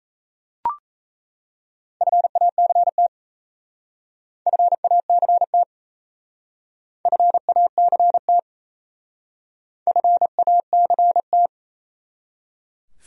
Morse code, heard as a keyed tone, about 35 wpm, 700 Hz. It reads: FACT FACT FACT FACT